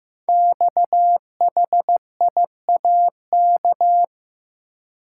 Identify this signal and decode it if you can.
Morse code, heard as a keyed tone, about 15 wpm, 700 Hz. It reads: XHIAK
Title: Code Groups - 5 Letter:Numbers - 15wpm